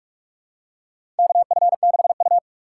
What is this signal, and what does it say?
Morse code, heard as a keyed tone, about 46 wpm, 700 Hz. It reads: KF6U
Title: USA Call Signs - 6x Character Spacing - 45wpm